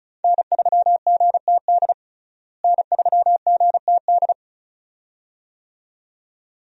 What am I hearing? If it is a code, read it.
Morse code, heard as a keyed tone, about 34 wpm, 700 Hz. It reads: N3GTD N3GTD
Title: USA Call Signs - 5 Characters - Repeated 2x - 35wpm